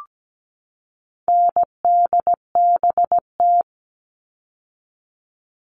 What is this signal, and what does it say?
Morse code, heard as a keyed tone, about 17 wpm, 700 Hz. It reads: NDBT
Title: All Cut Numbers - 4-Digit 17wpm